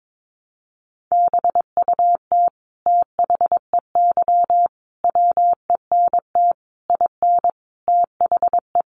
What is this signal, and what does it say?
Morse code, heard as a keyed tone, about 22 wpm, 700 Hz. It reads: BUT THEY WENT IN THE
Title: Sentences from Top 1300 Words 22wpm